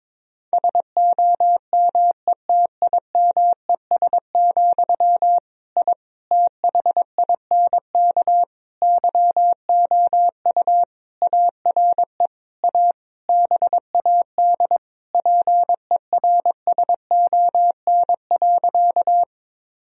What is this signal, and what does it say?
Morse code, heard as a keyed tone, about 22 wpm, 700 Hz. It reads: SOMETIMES, I THINK YOU ARE A BAD PERSON.
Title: Sentences from Top 1000 Words - Review Part 3 - 22wpm